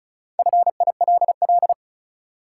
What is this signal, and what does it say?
Morse code, heard as a keyed tone, about 34 wpm, 700 Hz. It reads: FILL
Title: Sentences from Top 1000 Words - Review Part 3 - New Word Drill - 35wpm